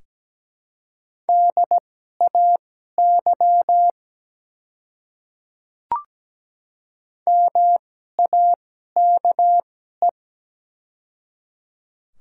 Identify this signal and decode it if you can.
Morse code, heard as a keyed tone, about 17 wpm, spaced as for 12 wpm, 700 Hz. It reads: DAY MAKE
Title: 2x Character Spacing - Top 100 Words - 17wpm